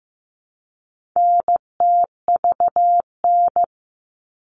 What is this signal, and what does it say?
Morse code, heard as a keyed tone, about 15 wpm, 700 Hz. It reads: NTVN
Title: All Cut Numbers - 4-Digit 15wpm